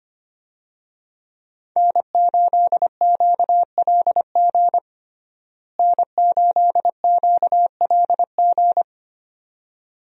Morse code, heard as a keyed tone, about 25 wpm, 700 Hz. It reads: N8QLG N8QLG
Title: USA Call Signs - 5 Characters - Repeated 2x - 25wpm